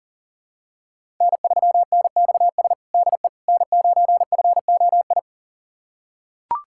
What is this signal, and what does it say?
Morse code, heard as a keyed tone, about 40 wpm, 700 Hz. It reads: N3NXS DE N9FOI
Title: USA Callsign DE Callsign - 5 Characters - 40wpm